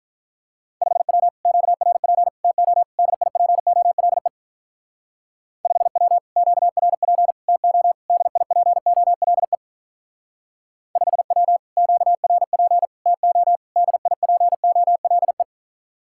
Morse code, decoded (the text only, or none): 5W QRP TO DIPOLE 5W QRP TO DIPOLE 5W QRP TO DIPOLE